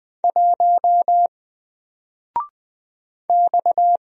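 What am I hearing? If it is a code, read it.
Morse code, heard as a keyed tone, about 20 wpm, 700 Hz. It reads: 1 X
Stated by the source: Single letter and number Morse drill at 20 wpm, from N4LCD.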